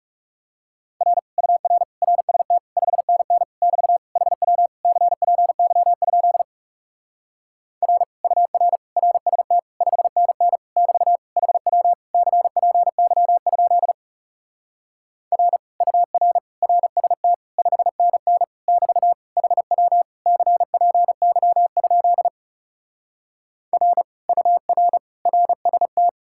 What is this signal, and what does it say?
Morse code, heard as a keyed tone, about 40 wpm, 700 Hz. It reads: R UR RST 5NN <BT> HW CPY? R UR RST 5NN <BT> HW CPY? R UR RST 5NN <BT> HW CPY? R UR RST